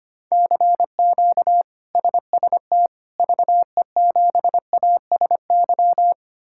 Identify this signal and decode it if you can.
Morse code, heard as a keyed tone, about 25 wpm, 700 Hz. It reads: CQ SST VE7ASY
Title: K1USN SST QSO Practice - 25wpm